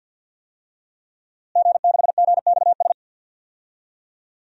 Morse code, heard as a keyed tone, about 53 wpm, 700 Hz. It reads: G6GXS
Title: DXCC-100 Call Signs - 5 Characters - Repeated 2x - 50wpm